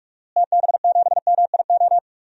Morse code, heard as a keyed tone, about 46 wpm, 700 Hz. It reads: TB7MIO
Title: DXCC-100 Call Signs - 6 Characters - Repeated 2x - 45wpm